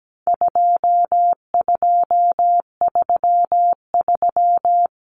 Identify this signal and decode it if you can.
Morse code, heard as a keyed tone, about 17 wpm, 700 Hz. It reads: 2233
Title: Common Cut Numbers - 4-Digit 17wpm